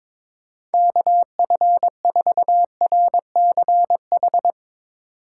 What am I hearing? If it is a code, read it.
Morse code, heard as a keyed tone, about 22 wpm, 700 Hz. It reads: KF4RCH
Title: Speed Racing - USA 2x3 Call Signs 22wpm